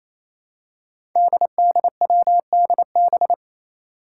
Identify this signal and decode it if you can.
Morse code, heard as a keyed tone, about 27 wpm, 700 Hz. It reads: DDWDB
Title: Five Challenging Letter-Number Combinations 28wpm